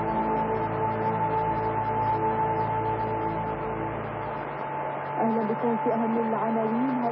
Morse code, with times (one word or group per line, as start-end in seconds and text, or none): none